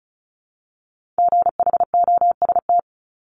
Morse code, heard as a keyed tone, about 36 wpm, 700 Hz. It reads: GHOST
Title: Multi-letter Phonemes Repeated 3x 35wpm